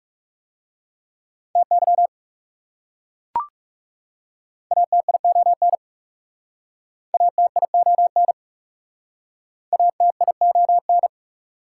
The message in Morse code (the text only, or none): TY ATION ATION ATION